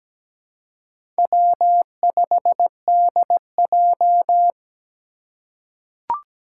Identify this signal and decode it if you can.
Morse code, heard as a keyed tone, about 17 wpm, 700 Hz. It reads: W5DJ
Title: Four Challenging Letter-Number Combinations 17wpm